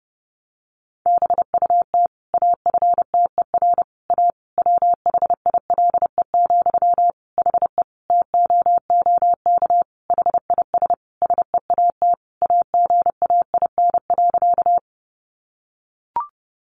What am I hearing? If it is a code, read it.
Morse code, heard as a keyed tone, about 30 wpm, 700 Hz. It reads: BUT AFTER A WHILE, HE TOOK HIS SEAT AGAIN.
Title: Sentences from Top 1500 Words 30wpm